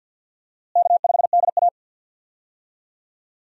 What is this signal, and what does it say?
Morse code, heard as a keyed tone, about 52 wpm, 700 Hz. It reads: K5DA
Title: USA Call Signs - 4 Characters - Repeated 2x - 50wpm